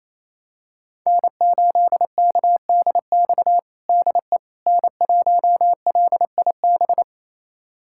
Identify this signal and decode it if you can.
Morse code, heard as a keyed tone, about 28 wpm, 700 Hz. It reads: N8KDX DE N1LIB
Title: USA Callsign DE Callsign - 5 Characters - 28wpm